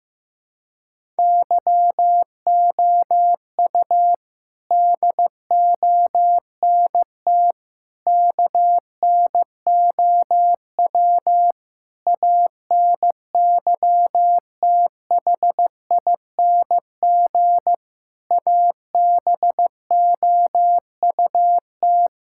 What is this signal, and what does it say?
Morse code, heard as a keyed tone, about 15 wpm, 700 Hz. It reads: YOU DONT KNOW ANYTHING ABOUT
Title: Sentences from Top 1000 Words - Review Part 6 - 15wpm